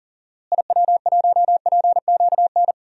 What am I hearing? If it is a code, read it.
Morse code, heard as a keyed tone, about 40 wpm, 700 Hz. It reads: IW1PQN
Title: DXCC-100 Call Signs - 6 Characters - Repeated 2x - 40wpm